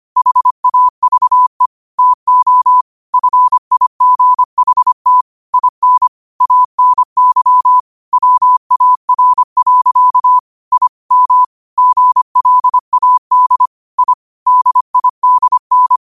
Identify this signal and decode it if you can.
Morse code, heard as a keyed tone, about 25 wpm, 1000 Hz. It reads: HAVE TO FIGHT IN ANY WAR. I M GLAD I DIDN
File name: 25wpm